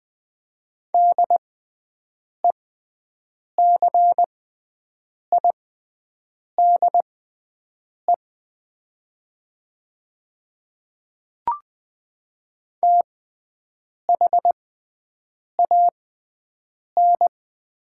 Morse code, heard as a keyed tone, about 20 wpm, spaced as for 7 wpm, 700 Hz. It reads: DECIDE THAN
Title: 6x Character Spacing - Top 500 Words - 20wpm